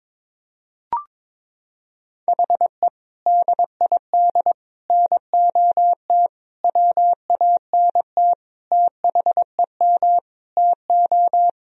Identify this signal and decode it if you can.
Morse code, heard as a keyed tone, about 22 wpm, 700 Hz. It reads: HE DID NOT WANT THEM TO
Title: Sentences from Top 1000 Words - Review Part 1 - 22wpm